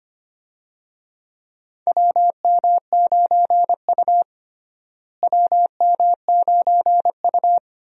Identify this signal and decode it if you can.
Morse code, heard as a keyed tone, about 25 wpm, 700 Hz. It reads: WM9U WM9U
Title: USA Call Signs - 4 Characters - Repeated 2x - 25wpm